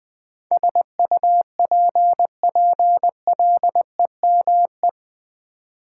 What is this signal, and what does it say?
Morse code, heard as a keyed tone, about 20 wpm, 700 Hz. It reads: SUPPLEME
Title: Build A Word - 4 Letters At A Time - 10-Letter Words - 20wpm